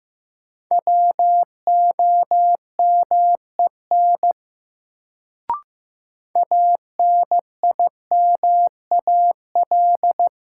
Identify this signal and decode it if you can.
Morse code, heard as a keyed tone, about 15 wpm, 700 Hz. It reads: WOMEN ANIMAL
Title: Sentences from Top 1300 Words - New Word Drill Alt - 15wpm